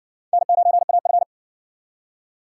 Morse code, heard as a keyed tone, about 53 wpm, 700 Hz. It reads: N9AF